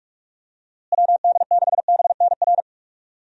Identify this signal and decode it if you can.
Morse code, heard as a keyed tone, about 46 wpm, 700 Hz. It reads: WD6BNR
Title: USA Call Signs - 6 Characters - Repeated 3x - 45wpm